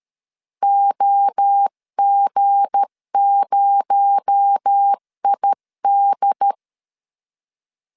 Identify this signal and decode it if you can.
Morse code, heard as a keyed tone, about 13 wpm, 800 Hz. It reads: OG0ID
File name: LEZIONI / 7 GF9Z / 7 G 12wpm 2055